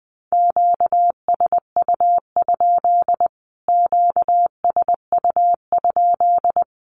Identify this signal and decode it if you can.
Morse code, heard as a keyed tone, about 20 wpm, 700 Hz. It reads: QSU? QSU?
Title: Q-Codes Repeated 3x 20wpm